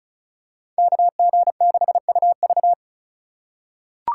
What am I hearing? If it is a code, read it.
Morse code, heard as a keyed tone, about 34 wpm, 700 Hz. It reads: KG6UV
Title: USA Call Signs - 5 Characters - 6x Character Spacing - 35wpm